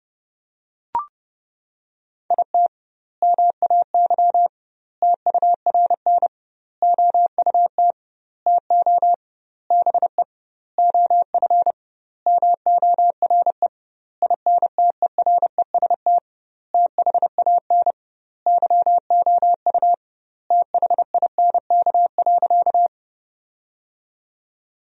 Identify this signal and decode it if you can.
Morse code, heard as a keyed tone, about 30 wpm, 700 Hz. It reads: IT MAY TURN OUT TO BE OF MORE INTEREST THAN YOU THINK.
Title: 2x Word-Spacing - Sentences from Top 400 Words - 30wpm